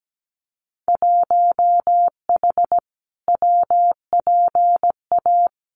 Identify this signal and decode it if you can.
Morse code, heard as a keyed tone, about 17 wpm, 700 Hz. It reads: 1H WPA
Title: Winter Field Day Exchanges 17wpm